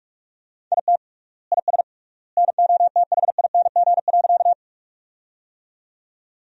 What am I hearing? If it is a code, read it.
Morse code, heard as a keyed tone, about 41 wpm, 700 Hz. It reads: IT IS NOTHING.
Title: 3x Word-Spacing - Sentences from Top 500 Words - 45wpm